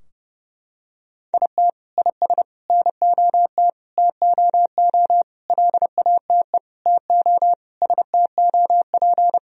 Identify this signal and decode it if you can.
Morse code, heard as a keyed tone, about 30 wpm, 700 Hz. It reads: IT IS NOT TOO LATE TO STOP
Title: Sentences from Top 1000 Words - Review Part 2 - 30wpm